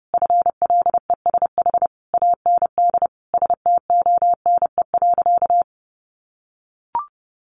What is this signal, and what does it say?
Morse code, heard as a keyed tone, about 30 wpm, 700 Hz. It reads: FLESH AND STONE.
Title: A Sound Of Thunder 30wpm